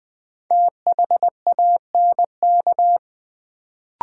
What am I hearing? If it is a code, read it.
Morse code, heard as a keyed tone, about 20 wpm, 700 Hz. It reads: THANK
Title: Sets of 2 Words - New Word Drill - 20wpm